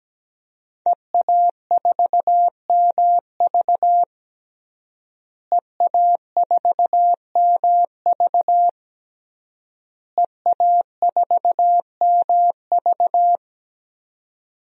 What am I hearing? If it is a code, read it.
Morse code, heard as a keyed tone, about 17 wpm, 700 Hz. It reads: EA4MV EA4MV EA4MV
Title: DXCC-100 Call Signs - 5 Characters - Repeated 3x - 17wpm